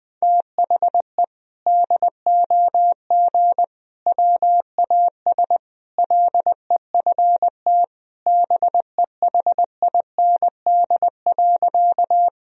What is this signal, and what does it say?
Morse code, heard as a keyed tone, about 20 wpm, 700 Hz. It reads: THE DOG WAS LEFT BEHIND.
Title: Sentences from Top 1000 Words - Review Part 2 - 20wpm